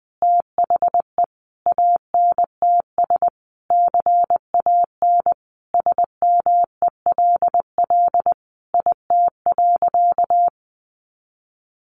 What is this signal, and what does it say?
Morse code, heard as a keyed tone, about 20 wpm, 700 Hz. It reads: THE ANTS CAN SMELL IT.
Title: Easy Phrases 20wpm